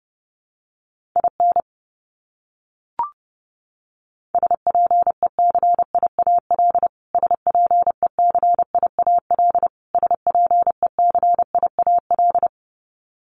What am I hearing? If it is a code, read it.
Morse code, heard as a keyed tone, about 30 wpm, 700 Hz. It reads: IN SPECIAL SPECIAL SPECIAL
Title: Top 500 Words Repeated 30wpm